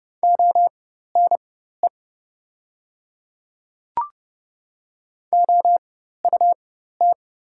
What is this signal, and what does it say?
Morse code, heard as a keyed tone, about 30 wpm, spaced as for 14 wpm, 690 Hz. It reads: ONE OUT
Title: 4x Character Spacing - Top 100 Words - 30wpm